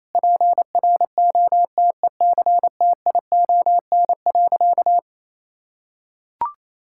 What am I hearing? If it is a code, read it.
Morse code, heard as a keyed tone, about 28 wpm, 700 Hz. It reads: PROTECTION.
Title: Sentences from Top 1600 Words 28wpm